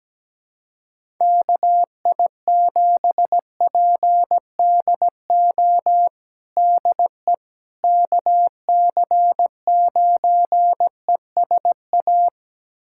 Morse code, heard as a keyed tone, about 17 wpm, 700 Hz. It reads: KI7PDO DE KC9ESA
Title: USA Callsign DE Callsign - 6 Characters - 17wpm